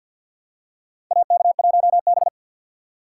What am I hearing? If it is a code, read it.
Morse code, heard as a keyed tone, about 52 wpm, 700 Hz. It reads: AK1B
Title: USA Call Signs - 2x Character Spacing - 50wpm